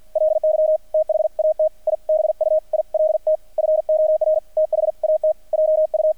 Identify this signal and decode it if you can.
Morse code, heard as a keyed tone, about 48 wpm, 620 Hz, with radio noise. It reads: LY THAT I BUILT FOR THAT JU
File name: archived / CRSnet